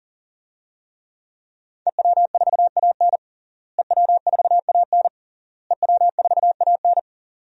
Morse code, heard as a keyed tone, about 40 wpm, 700 Hz. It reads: EW4AN EW4AN EW4AN
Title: DXCC-100 Call Signs - 5 Characters - Repeated 3x - 40wpm